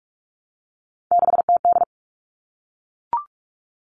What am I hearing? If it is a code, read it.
Morse code, heard as a keyed tone, about 44 wpm, 700 Hz. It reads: 6TD
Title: All Cut Numbers - 3-Digit 45wpm